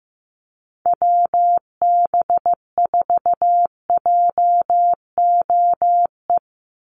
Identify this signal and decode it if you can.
Morse code, heard as a keyed tone, about 15 wpm, 700 Hz. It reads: WB4JOE
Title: USA Call Signs 15wpm